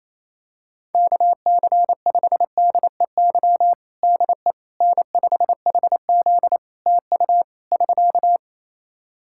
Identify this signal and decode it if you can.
Morse code, heard as a keyed tone, about 28 wpm, 700 Hz. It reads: KC5DEY DE N5HZ TU <SK>
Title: Practice QSO - Exchange 4 - Ending - 28wpm